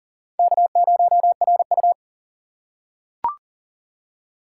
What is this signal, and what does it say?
Morse code, heard as a keyed tone, about 40 wpm, 700 Hz. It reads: K0RU
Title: K1USN SST QSO Practice - 40wpm